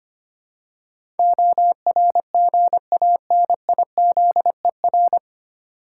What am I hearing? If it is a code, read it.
Morse code, heard as a keyed tone, about 25 wpm, 700 Hz. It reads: ORGANIZER
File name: Nine Letter Words - 25wpm